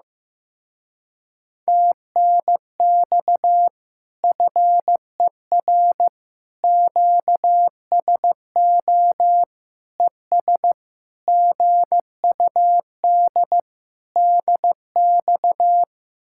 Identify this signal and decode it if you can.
Morse code, heard as a keyed tone, about 15 wpm, 700 Hz. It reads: TNX FER QSO ES GUD DX
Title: Speed-Racing Common QSO Phrases Encore - 15wpm